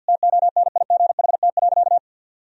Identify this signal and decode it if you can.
Morse code, heard as a keyed tone, about 51 wpm, 700 Hz. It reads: TONIGHT.
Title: Sentences from Top 1000 Words - Review Part 3 - 50wpm